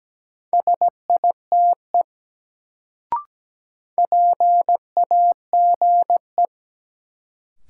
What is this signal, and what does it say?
Morse code, heard as a keyed tone, about 17 wpm, 700 Hz. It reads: SITE PAGE
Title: Sentences from Top 1000 Words - Review Part 1 - New Word Drill - 17wpm